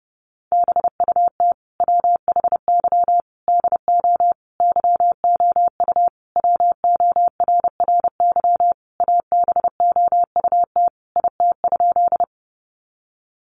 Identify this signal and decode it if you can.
Morse code, heard as a keyed tone, about 28 wpm, 700 Hz. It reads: BUT WHY DO YOU WORRY ABOUT IT?
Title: Sentences from Top 1100 Words 30wpm